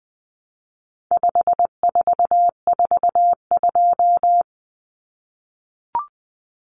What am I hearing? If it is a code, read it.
Morse code, heard as a keyed tone, about 20 wpm, 700 Hz. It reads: 5442
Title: Numbers - 4-Digit 20wpm